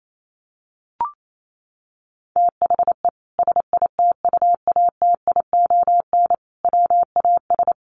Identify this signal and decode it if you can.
Morse code, heard as a keyed tone, about 28 wpm, 700 Hz. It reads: THE SITUATION WAS
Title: Sentences from Top 1300 Words 28wpm